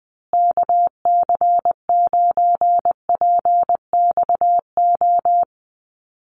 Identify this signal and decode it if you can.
Morse code, heard as a keyed tone, about 20 wpm, 700 Hz. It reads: KC9PXO
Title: USA Call Signs Repeated 3x 20wpm